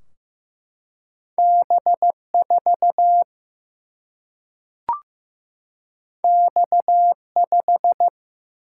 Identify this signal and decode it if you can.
Morse code, heard as a keyed tone, about 15 wpm, 700 Hz. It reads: B4 X5
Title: Single-Letter Single-Number 15wpm